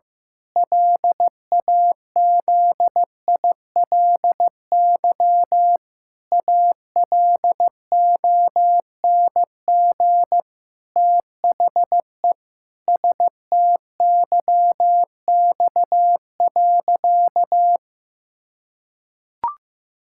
Morse code, heard as a keyed tone, about 16 wpm, 700 Hz. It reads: LAZILY ALONG THE STYX.
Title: Pangram Challenge v2 - 15wpm